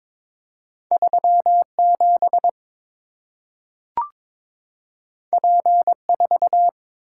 Morse code, heard as a keyed tone, about 23 wpm, 700 Hz. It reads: 37 P4
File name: practice / 22wpm / Two Letter-Number Combinations - 22wpm